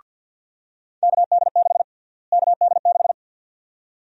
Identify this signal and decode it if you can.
Morse code, heard as a keyed tone, about 52 wpm, 700 Hz. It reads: KD6 KD6
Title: USA Call Sign Prefixes - Repeated 2x - 50wpm